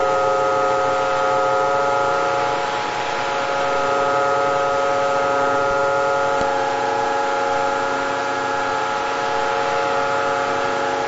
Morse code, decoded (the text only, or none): none